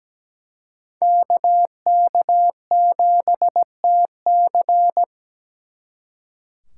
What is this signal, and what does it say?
Morse code, heard as a keyed tone, about 20 wpm, 700 Hz. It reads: KK7TC
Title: Speed Racing - USA 2x2 Call Signs 17wpm